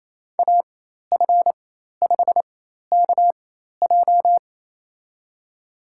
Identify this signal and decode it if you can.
Morse code, heard as a keyed tone, about 28 wpm, spaced as for 13 wpm, 700 Hz. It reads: AF5KJ
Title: USA Call Signs - 5 Characters - 4x Character Spacing - 28wpm